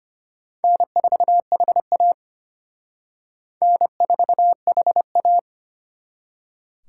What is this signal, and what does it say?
Morse code, heard as a keyed tone, about 31 wpm, 700 Hz. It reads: N4HA N4HA